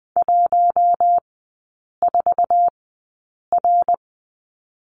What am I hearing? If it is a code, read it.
Morse code, heard as a keyed tone, about 20 wpm, spaced as for 8 wpm, 690 Hz. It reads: 14R
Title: 145 Lesson 24e - Letter y - Three Characters 20wpm